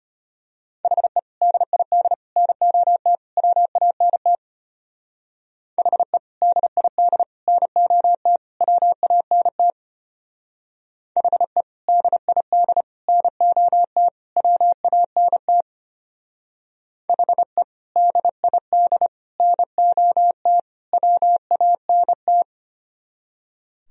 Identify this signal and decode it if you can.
Morse code, heard as a keyed tone, about 33 wpm, 700 Hz. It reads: HE DID NOT WANT HE DID NOT WANT HE DID NOT WANT HE DID NOT WANT